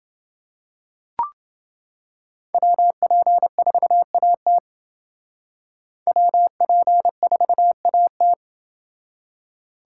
Morse code, heard as a keyed tone, about 29 wpm, 700 Hz. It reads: WP4AT WP4AT